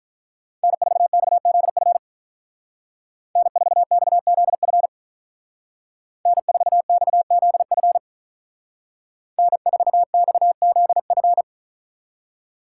Morse code, heard as a keyed tone, about 44 wpm, 700 Hz. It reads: N4XZF N4XZF N4XZF N4XZF